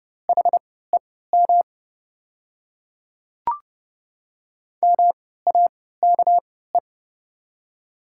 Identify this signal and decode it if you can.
Morse code, heard as a keyed tone, about 30 wpm, spaced as for 17 wpm, 700 Hz. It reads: HEM MAKE
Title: 3x Character Spacing - Top 300 Words - 30wpm